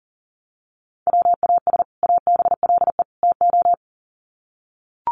Morse code, heard as a keyed tone, about 40 wpm, 700 Hz. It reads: WAS ABLE TO
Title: Sets of 3 Words 40wpm